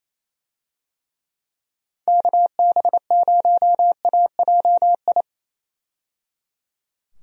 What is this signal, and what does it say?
Morse code, heard as a keyed tone, about 27 wpm, 700 Hz. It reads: KB0AJI
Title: USA Call Signs - 6 Characters - 28wpm